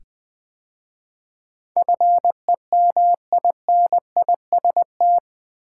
Morse code, heard as a keyed tone, about 20 wpm, 700 Hz. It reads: FEMINIST
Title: 8 Letter Words - First 3 Letters Given - 20wpm